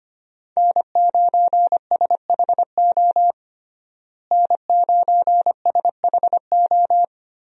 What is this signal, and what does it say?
Morse code, heard as a keyed tone, about 25 wpm, 700 Hz. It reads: N9SHO N9SHO
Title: USA Call Signs - 5 Characters - Repeated 2x - 25wpm